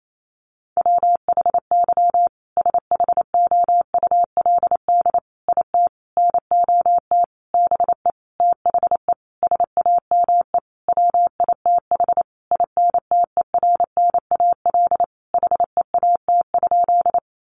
Morse code, heard as a keyed tone, about 28 wpm, 700 Hz. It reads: WHY SHOULD IT NOT BE THE SAME WITH INTERNAL HEAT?
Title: Sentences from Top 1400 Words 28wpm